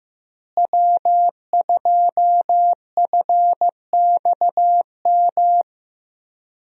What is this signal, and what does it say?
Morse code, heard as a keyed tone, about 15 wpm, 700 Hz. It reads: W2FXM
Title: USA Call Signs - 5 Characters - Repeated 2x - 15wpm